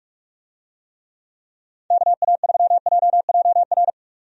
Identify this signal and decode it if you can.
Morse code, heard as a keyed tone, about 46 wpm, 700 Hz. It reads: KA3JJR
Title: USA Call Signs - 6 Characters - 45wpm